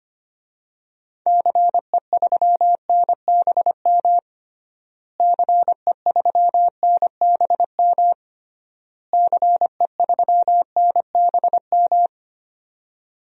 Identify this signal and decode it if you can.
Morse code, heard as a keyed tone, about 25 wpm, 700 Hz. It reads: CE3NBM CE3NBM CE3NBM
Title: DXCC-100 Call Signs - 6 Characters - Repeated 3x - 25wpm